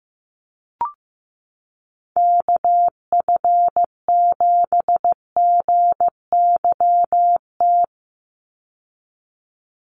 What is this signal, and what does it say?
Morse code, heard as a keyed tone, about 15 wpm, 700 Hz. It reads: KF7GYT
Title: USA Call Signs 15wpm